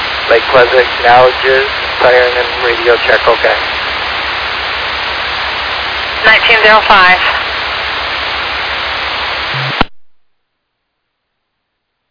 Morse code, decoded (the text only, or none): none